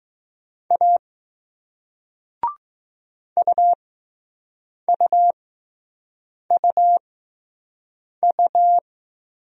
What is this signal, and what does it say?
Morse code, heard as a keyed tone, about 20 wpm, spaced as for 6 wpm, 700 Hz. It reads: A UUUU